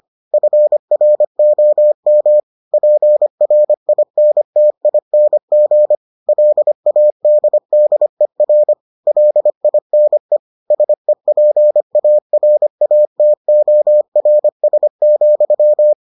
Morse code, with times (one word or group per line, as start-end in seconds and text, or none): 0.3-2.4 FROM
2.7-6.0 PRINTING
6.3-8.7 LADDER
9.1-10.4 LINE
10.7-15.9 SEPARATORS,